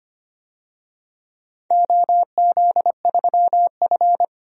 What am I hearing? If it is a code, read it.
Morse code, heard as a keyed tone, about 25 wpm, 700 Hz. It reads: OZ3F
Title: DXCC-100 Call Signs - 4 Characters - 25wpm